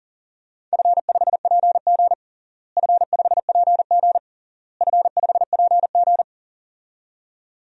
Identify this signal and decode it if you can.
Morse code, heard as a keyed tone, about 40 wpm, 700 Hz. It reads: F5PG F5PG F5PG
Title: DXCC-100 Call Signs - 4 Characters - Repeated 3x - 40wpm